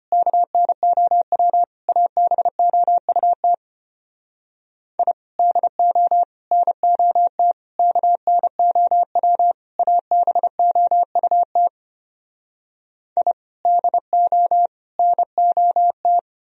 Morse code, heard as a keyed tone, about 33 wpm, 700 Hz. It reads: KNOW ABOUT I DO NOT KNOW ABOUT I DO NOT